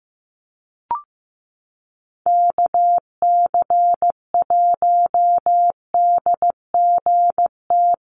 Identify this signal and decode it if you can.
Morse code, heard as a keyed tone, about 15 wpm, 700 Hz. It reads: KC1DGT
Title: USA Call Signs Repeated 3x 15wpm